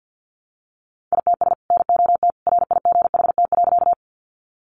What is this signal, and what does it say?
Morse code, heard as a keyed tone, about 48 wpm, 700 Hz. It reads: ITS NOT RIGHT.
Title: Sentences from Top 600 Words 50wpm